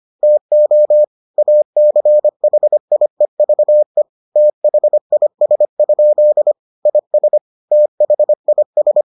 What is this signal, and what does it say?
Morse code, heard as a keyed tone, about 25 wpm, 600 Hz. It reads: TO ACHIEVE THIS? IS THIS